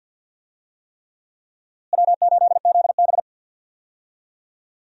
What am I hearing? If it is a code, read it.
Morse code, heard as a keyed tone, about 50 wpm, 700 Hz. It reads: W8ZB
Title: USA Call Signs - 4 Characters - Repeated 3x - 50wpm